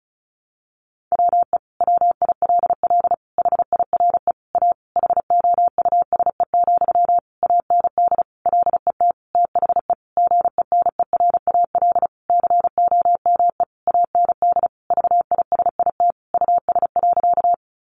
Morse code, heard as a keyed tone, about 34 wpm, 700 Hz. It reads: WE WILL HIRE A HOUSE, AND LET THE GENERAL COME AND VISIT US.
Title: Sentences from Top 1400 Words 35wpm